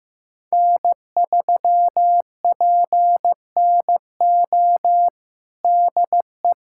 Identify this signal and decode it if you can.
Morse code, heard as a keyed tone, about 15 wpm, 700 Hz. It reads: N3PNO DE
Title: USA Callsign DE Callsign - 5 Characters - 15wpm